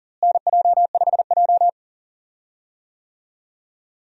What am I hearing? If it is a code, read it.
Morse code, heard as a keyed tone, about 38 wpm, 700 Hz. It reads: NJ5J
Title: USA Call Signs - 4 Characters - Repeated 3x - 40wpm